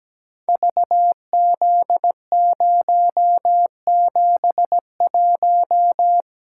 Morse code, heard as a keyed tone, about 17 wpm, 700 Hz. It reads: VZ071
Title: Code Groups - 5 Letter:Numbers - 17wpm